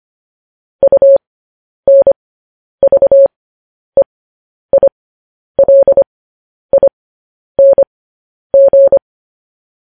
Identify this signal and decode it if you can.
Morse code, heard as a keyed tone, about 25 wpm, spaced as for 10 wpm, 550 Hz. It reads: UNVEILING
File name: podcast / News headlines-1x-25x10